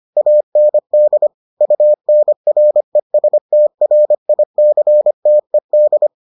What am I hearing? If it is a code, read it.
Morse code, heard as a keyed tone, about 25 wpm, 600 Hz. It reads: AND UNRESTRICTED